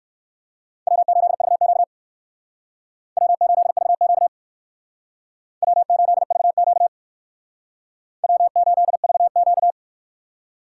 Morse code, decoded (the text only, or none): W8VQ W8VQ W8VQ W8VQ